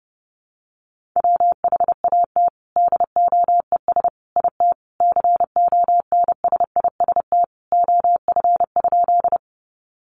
Morse code, heard as a keyed tone, about 30 wpm, 700 Hz. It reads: WHAT DOES IT CONSIST OF?
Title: Sentences from Top 1400 Words 30wpm